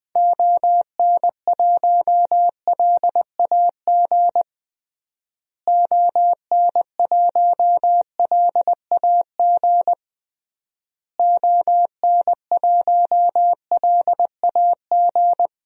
Morse code, heard as a keyed tone, about 20 wpm, 700 Hz. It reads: ON1LAG ON1LAG ON1LAG
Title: DXCC-100 Call Signs - 6 Characters - Repeated 3x - 20wpm